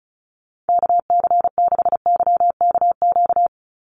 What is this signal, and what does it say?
Morse code, heard as a keyed tone, about 34 wpm, 700 Hz. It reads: KC6YKQ
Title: USA Call Signs 35wpm